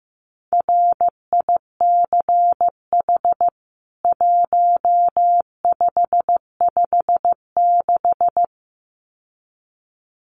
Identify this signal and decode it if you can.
Morse code, heard as a keyed tone, about 15 wpm, 700 Hz. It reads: RICH 1556
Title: CWT Exchange 15wpm